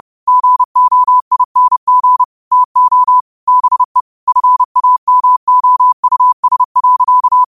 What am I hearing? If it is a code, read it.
Morse code, heard as a keyed tone, about 30 wpm, 1000 Hz. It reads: GOING TO BE FAMOUS.
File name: qotdfiles / 30wpm